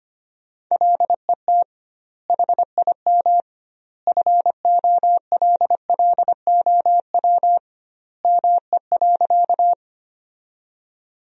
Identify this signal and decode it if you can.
Morse code, heard as a keyed tone, about 25 wpm, 700 Hz. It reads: LET HIM FOLLOW ME.
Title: 2x Word-Spacing - Sentences from Top 300 Words - 25wpm